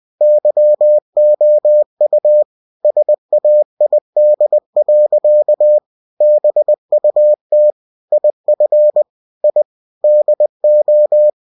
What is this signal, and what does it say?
Morse code, heard as a keyed tone, about 20 wpm, 600 Hz. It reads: YOU SAID. BUT IF I DO